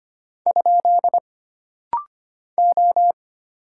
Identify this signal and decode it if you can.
Morse code, heard as a keyed tone, about 25 wpm, 700 Hz. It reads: ? O
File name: Single Character (Letter:Number:Punctuation:Prosign) - Rapid-Fire - 25wpm